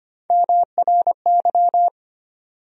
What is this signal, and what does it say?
Morse code, heard as a keyed tone, about 25 wpm, 700 Hz. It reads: MRY
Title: USA Call Sign Postfixes - 3 Letters 25wpm